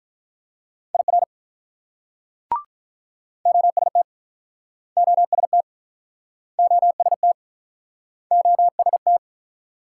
Morse code, heard as a keyed tone, about 45 wpm, 700 Hz. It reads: IR OST OST OST OST